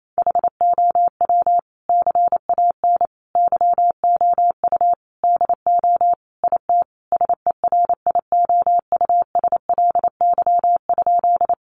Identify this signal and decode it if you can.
Morse code, heard as a keyed tone, about 28 wpm, 700 Hz. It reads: HOW CAN YOU DO IT SERIOUSLY?
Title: Sentences from Top 1200 Words 28wpm